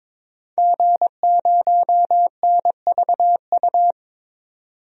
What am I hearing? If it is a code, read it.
Morse code, heard as a keyed tone, about 22 wpm, 700 Hz. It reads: G0NVU
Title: DXCC-100 Call Signs - 5 Characters - 22wpm